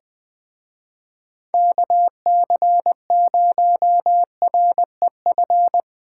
Morse code, heard as a keyed tone, about 22 wpm, 700 Hz. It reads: KC0REF
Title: Speed Racing - USA 2x3 Call Signs 20wpm